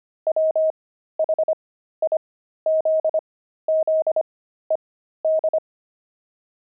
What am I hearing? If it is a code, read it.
Morse code, heard as a keyed tone, about 25 wpm, spaced as for 13 wpm, 630 Hz. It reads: WHIZZED
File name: pangram-fs-13-cs-25-javelins